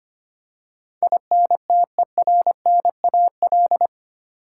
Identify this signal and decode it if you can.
Morse code, heard as a keyed tone, about 25 wpm, 700 Hz. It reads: INTERNAL
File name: 8 Letter Words - First 3 Letters Given - 25wpm